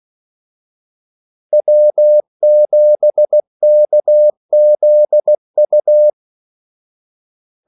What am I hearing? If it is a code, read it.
Morse code, heard as a keyed tone, about 16 wpm, 600 Hz. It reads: W7KZU
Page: Kaldesignaler 16 wpm